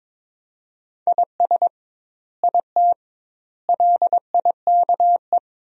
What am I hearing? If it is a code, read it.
Morse code, heard as a keyed tone, about 22 wpm, 700 Hz. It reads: IS IT LIKE
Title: 2x Word-Spacing - Sentences from Top 100 Words - 22wpm